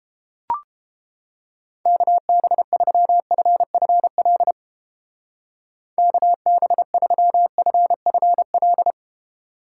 Morse code, heard as a keyed tone, about 31 wpm, 700 Hz. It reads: KB3FFL KB3FFL